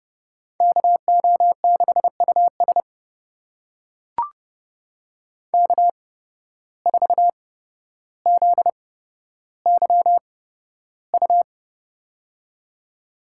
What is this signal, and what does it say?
Morse code, heard as a keyed tone, about 30 wpm, spaced as for 13 wpm, 700 Hz. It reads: KO6US K4ZYU
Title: USA Call Signs - 5 Characters - 8x Character Spacing - 30wpm